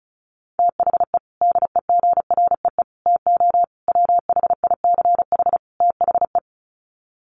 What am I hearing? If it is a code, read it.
Morse code, heard as a keyed tone, about 34 wpm, 700 Hz. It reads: THE DEGREE TO WHICH THE
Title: Sets of 5 Words Encore2 35wpm